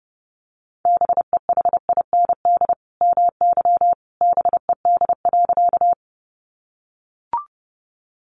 Morse code, heard as a keyed tone, about 30 wpm, 700 Hz. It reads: BEHIND MY BED.
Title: Sentences from Top 500 Words 30wpm